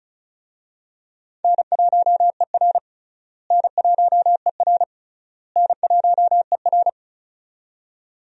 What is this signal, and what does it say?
Morse code, heard as a keyed tone, about 34 wpm, 700 Hz. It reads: N1ER N1ER N1ER
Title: USA Call Signs - 4 Characters - Repeated 3x - 35wpm